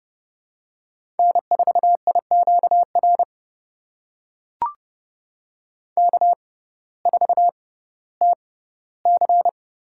Morse code, heard as a keyed tone, about 31 wpm, spaced as for 17 wpm, 700 Hz. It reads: N4IQR K4TC
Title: USA Call Signs - 6x Character Spacing - 30wpm